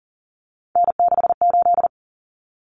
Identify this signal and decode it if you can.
Morse code, heard as a keyed tone, about 40 wpm, 700 Hz. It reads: N68
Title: Common Cut Numbers - 3-Digit 40wpm